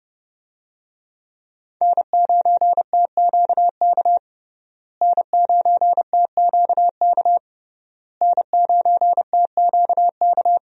Morse code, heard as a keyed tone, about 29 wpm, 700 Hz. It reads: N9TQK N9TQK N9TQK
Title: USA Call Signs - 5 Characters - Repeated 3x - 30wpm